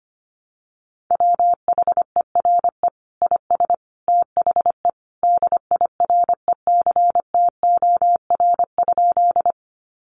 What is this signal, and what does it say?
Morse code, heard as a keyed tone, about 25 wpm, 700 Hz. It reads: WHERE IS THE DIRECTOR?
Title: Sentences from Top 1300 Words 25wpm